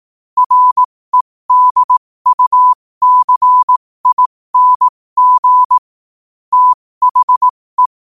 Morse code, heard as a keyed tone, about 18 wpm, spaced as for 15 wpm, 1000 Hz. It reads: REDUCING THE
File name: qotdfiles / 15wpm